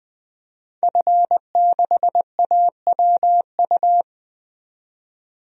Morse code, heard as a keyed tone, about 20 wpm, 700 Hz. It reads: F6AWU
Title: DXCC-100 Call Signs - 5 Characters - 20wpm